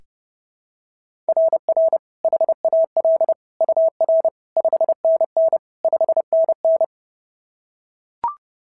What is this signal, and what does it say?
Morse code, heard as a keyed tone, about 30 wpm, 690 Hz. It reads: RR HAL UR 5NN 5NN
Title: SOTA QSO Practice 30wpm